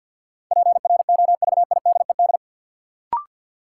Morse code, heard as a keyed tone, about 53 wpm, 700 Hz. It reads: PROVIDED
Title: Sentences from Top 1500 Words - New Word Drill - 50wpm